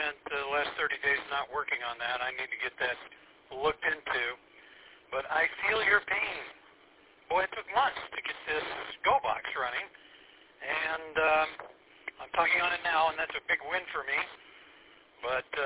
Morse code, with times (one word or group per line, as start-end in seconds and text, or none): none